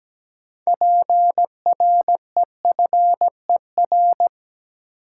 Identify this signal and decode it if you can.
Morse code, heard as a keyed tone, about 17 wpm, 700 Hz. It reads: PREFER
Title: Sentences from Top 1100 Words - New Word Drill - 17wpm